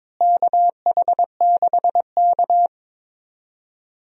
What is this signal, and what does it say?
Morse code, heard as a keyed tone, about 22 wpm, 700 Hz. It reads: KH6K
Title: DXCC-100 Call Signs - 4 Characters - Repeated 3x - 22wpm